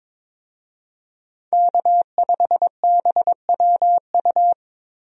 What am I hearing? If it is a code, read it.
Morse code, heard as a keyed tone, about 22 wpm, 700 Hz. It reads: K5BWU
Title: USA Call Signs - 5 Characters - Repeated 2x - 22wpm